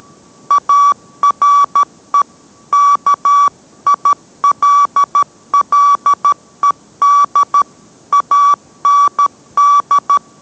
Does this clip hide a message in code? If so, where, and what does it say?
Yes: Morse code from 0.5 to 10.2 s, heard as a keyed tone, about 13 wpm, 1150 Hz. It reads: ARE KILLED AND